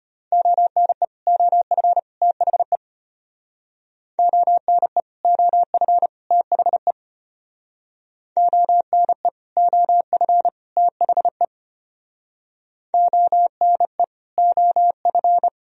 Morse code, heard as a keyed tone, about 33 wpm, 700 Hz. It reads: ONE OF THE ONE OF THE ONE OF THE ONE OF